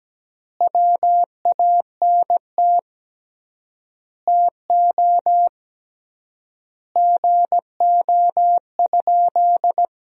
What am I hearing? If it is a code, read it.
Morse code, heard as a keyed tone, about 17 wpm, 700 Hz. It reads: WANT TO GO?
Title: 3x Word-Spacing - Sentences from Top 300 Words - 17wpm